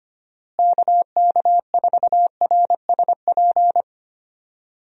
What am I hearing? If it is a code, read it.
Morse code, heard as a keyed tone, about 25 wpm, 700 Hz. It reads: KK4RSP
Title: USA Call Signs - 6 Characters - Repeated 2x - 25wpm